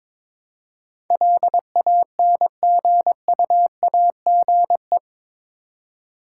Morse code, heard as a keyed tone, about 22 wpm, 700 Hz. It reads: LANGUAGE
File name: practice / 22wpm / Eight Letter Words - 22wpm